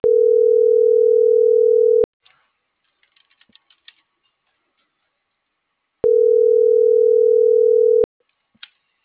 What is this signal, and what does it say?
Ringback tone (North America), 440 + 480 Hz.